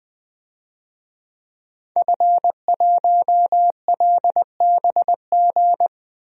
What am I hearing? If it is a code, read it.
Morse code, heard as a keyed tone, about 21 wpm, 700 Hz. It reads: F1LBG
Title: DXCC-100 Call Signs - 5 Characters - 20wpm